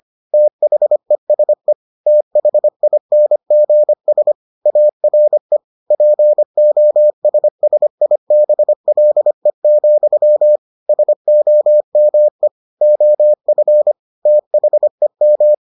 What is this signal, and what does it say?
Morse code, heard as a keyed tone, about 25 wpm, 600 Hz. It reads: THESE THINGS ARE POSSIBLE, SOME OF THEM